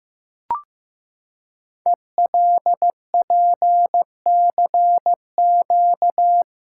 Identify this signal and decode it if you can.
Morse code, heard as a keyed tone, about 15 wpm, 700 Hz. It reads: ELPCQ
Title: Code Groups - 5 Letters - 15wpm